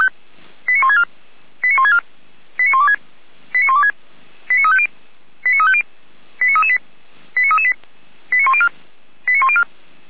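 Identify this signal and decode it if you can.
Five-tone selective call (ZVEI), heard as quick runs of five tones, tones of about 70 ms. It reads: 89164 89164 89126 89126 89350 89350 89338 89338 89104 89104
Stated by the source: ZVEI five-tone selective call 89701, received by SDR.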